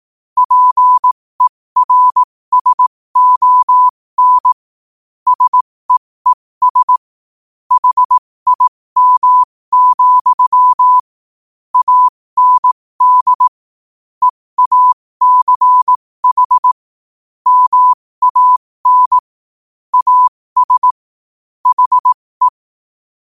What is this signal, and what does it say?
Morse code, heard as a keyed tone, about 18 wpm, spaced as for 15 wpm, 1000 Hz. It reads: PERSON SEES HIM, AND EACH MAN AS HE
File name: qotdfiles / 15wpm